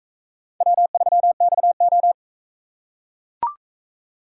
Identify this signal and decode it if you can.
Morse code, heard as a keyed tone, about 41 wpm, 700 Hz. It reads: W3XO